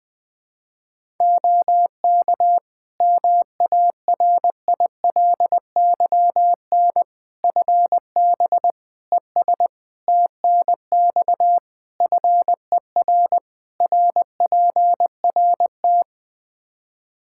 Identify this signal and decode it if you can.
Morse code, heard as a keyed tone, about 20 wpm, 700 Hz. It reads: OK MARILYN FB ES TNX FER RPRT
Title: Practice QSO - Exchange 2 - Info - 20wpm